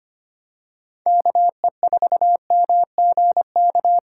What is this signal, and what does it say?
Morse code, heard as a keyed tone, about 25 wpm, 700 Hz. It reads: KE4MGK
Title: USA Call Signs - 6 Characters - 2x Character Spacing - 25wpm